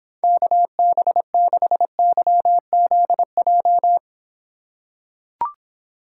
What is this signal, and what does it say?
Morse code, heard as a keyed tone, about 26 wpm, 700 Hz. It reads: KB6YZJ